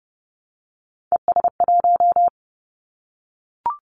Morse code, heard as a keyed tone, about 30 wpm, 700 Hz. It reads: ES1
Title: Callsign Prefixes for Top 50 DXCC - 30wpm